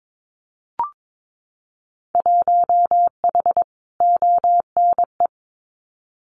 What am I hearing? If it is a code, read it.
Morse code, heard as a keyed tone, about 22 wpm, 700 Hz. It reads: 1H ONE
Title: Winter Field Day Exchanges 22wpm